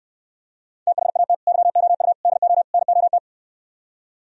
Morse code, heard as a keyed tone, about 74 wpm, 700 Hz. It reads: THAT YOU DO NOT